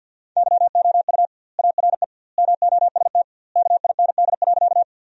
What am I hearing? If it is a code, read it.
Morse code, heard as a keyed tone, about 53 wpm, 700 Hz. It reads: YOU ARE MOST KIND.
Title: 2x Word-Spacing - Sentences from Top 200 Words - 50wpm